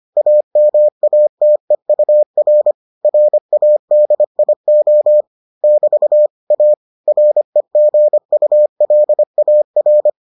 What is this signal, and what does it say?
Morse code, heard as a keyed tone, about 25 wpm, 600 Hz. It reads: AMATEUR RADIO <BT> A REGULAR